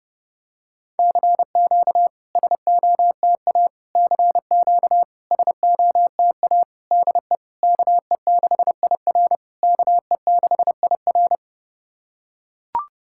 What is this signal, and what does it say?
Morse code, heard as a keyed tone, about 30 wpm, 690 Hz. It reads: CQ SOTA CQ SOTA DE KE6IR KE6IR
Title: SOTA QSO Practice 30wpm